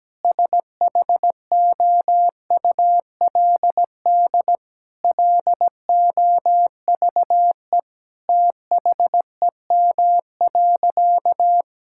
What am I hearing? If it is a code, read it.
Morse code, heard as a keyed tone, about 17 wpm, 700 Hz. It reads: SHOULD LOVE THEM.
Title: Sentences from Top 1600 Words 17wpm